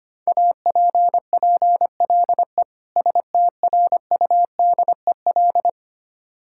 Morse code, heard as a keyed tone, about 25 wpm, 700 Hz. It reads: APPLE STRUDEL
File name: practice / 25wpm / Difficult Words with Hints - 25wpm